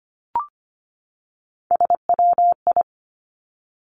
Morse code, heard as a keyed tone, about 25 wpm, 700 Hz. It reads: SWI
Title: Three Challenging Letter-Number Combinations 25wpm